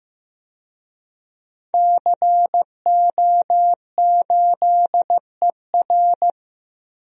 Morse code, heard as a keyed tone, about 15 wpm, 700 Hz. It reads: CO8ER
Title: DXCC-100 Call Signs - 5 Characters - Repeated 2x - 15wpm